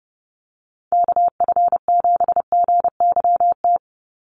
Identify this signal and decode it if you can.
Morse code, heard as a keyed tone, about 30 wpm, 700 Hz. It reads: KF7GYT
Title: USA Call Signs 30wpm